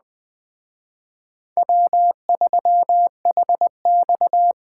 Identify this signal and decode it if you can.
Morse code, heard as a keyed tone, about 20 wpm, 700 Hz. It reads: W3HX